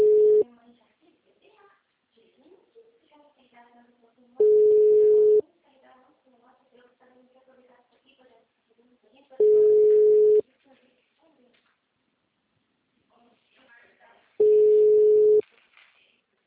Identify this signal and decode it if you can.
Ringback tone (Europe, 425 Hz), with speech.
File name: out- -2024- - - .0